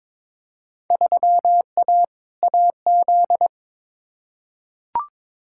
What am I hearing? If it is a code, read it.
Morse code, heard as a keyed tone, about 22 wpm, 700 Hz. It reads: 3A AZ
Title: Field Day Exchanges Encore - 22wpm